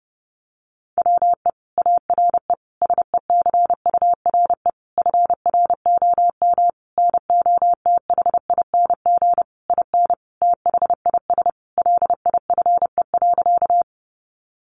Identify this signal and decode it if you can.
Morse code, heard as a keyed tone, about 30 wpm, 700 Hz. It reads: WE ARE SECURE FROM NOTHING IN THIS LIFE.
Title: Sentences from Top 1200 Words 30wpm